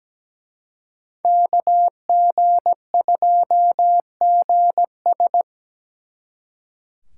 Morse code, heard as a keyed tone, about 20 wpm, 700 Hz. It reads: KG2GS
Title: Speed Racing - USA 2x2 Call Signs 17wpm